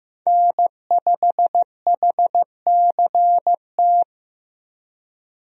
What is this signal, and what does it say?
Morse code, heard as a keyed tone, about 16 wpm, 700 Hz. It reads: N5HCT
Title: USA Call Signs - 5 Characters - Repeated 3x - 15wpm